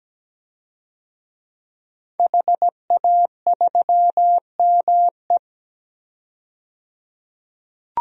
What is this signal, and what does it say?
Morse code, heard as a keyed tone, about 17 wpm, 700 Hz. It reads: HA3ME
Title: DXCC-100 Call Signs - 5 Characters - 17wpm